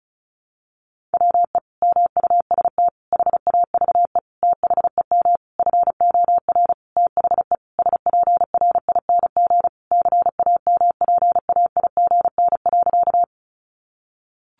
Morse code, heard as a keyed tone, about 34 wpm, 700 Hz. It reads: WE MUST HAVE THEM FOR THE SPRING CAMPAIGN.
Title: Sentences from Top 1200 Words 35wpm